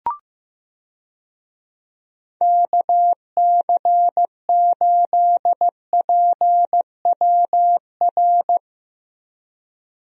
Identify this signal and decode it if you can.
Morse code, heard as a keyed tone, about 15 wpm, 700 Hz. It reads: KC8PWR
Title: USA Call Signs - 6 Characters - Repeated 3x - 15wpm